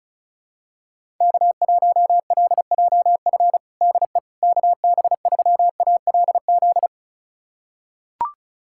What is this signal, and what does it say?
Morse code, heard as a keyed tone, about 36 wpm, 700 Hz. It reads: K1LJF DE KB3ALZ
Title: USA Callsign DE Callsign 35wpm